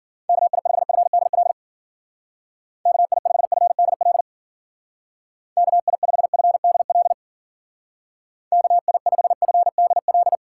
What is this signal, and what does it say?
Morse code, heard as a keyed tone, about 50 wpm, 700 Hz. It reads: KI5FDL KI5FDL KI5FDL KI5FDL